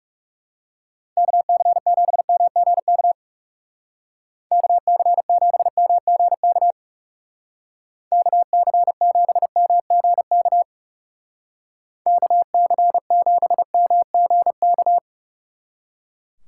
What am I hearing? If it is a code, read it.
Morse code, heard as a keyed tone, about 38 wpm, 700 Hz. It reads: KC7MGK KC7MGK KC7MGK KC7MGK